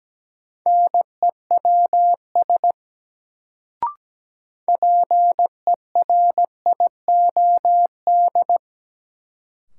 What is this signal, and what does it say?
Morse code, heard as a keyed tone, about 17 wpm, 700 Hz. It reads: NEWS PERIOD
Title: Sentences from Top 800 Words - New Word Drill - 17wpm